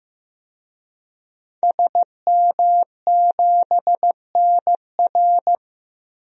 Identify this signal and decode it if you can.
Morse code, heard as a keyed tone, about 15 wpm, 700 Hz. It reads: SM7NR
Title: DXCC-100 Call Signs - 5 Characters - Repeated 3x - 15wpm